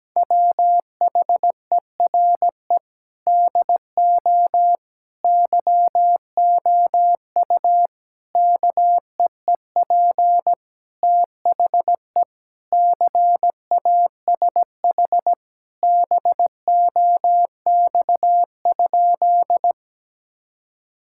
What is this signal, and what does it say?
Morse code, heard as a keyed tone, about 17 wpm, 700 Hz. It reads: WHERE DO YOU KEEP THE CASH BOX?
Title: Sentences from Top 1000 Words - Review Part 5 - 17wpm